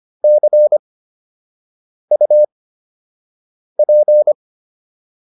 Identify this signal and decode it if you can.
Morse code, heard as a keyed tone, about 25 wpm, spaced as for 6 wpm, 600 Hz. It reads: CUP